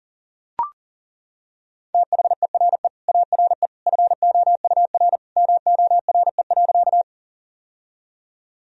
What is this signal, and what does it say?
Morse code, heard as a keyed tone, about 40 wpm, 700 Hz. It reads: THERE ARE FOUR MORE.
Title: Sentences from Top 1000 Words - Review Part 1 - 40wpm